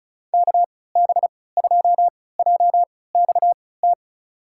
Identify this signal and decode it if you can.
Morse code, heard as a keyed tone, about 36 wpm, spaced as for 20 wpm, 700 Hz. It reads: KB2JXT
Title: USA Call Signs - 6 Characters - 3x Character Spacing - 35wpm